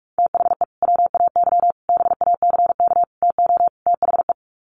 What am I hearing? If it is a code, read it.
Morse code, heard as a keyed tone, about 44 wpm, 700 Hz. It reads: THE WAY BACK TO THE
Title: Sets of 5 Words Encore2 45wpm